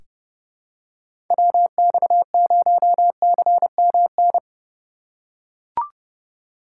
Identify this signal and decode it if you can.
Morse code, heard as a keyed tone, about 30 wpm, 700 Hz. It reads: WX0CMN
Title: USA Call Signs - 6 Characters - 4x Character Spacing - 30wpm